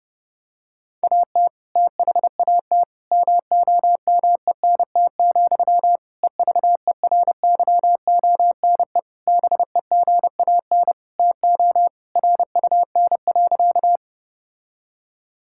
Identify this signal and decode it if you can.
Morse code, heard as a keyed tone, about 30 wpm, 700 Hz. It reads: AT THAT MOMENT, EVERYONE BEGAN TO RUN.
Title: Sentences from Top 1000 Words - Review Part 5 - 30wpm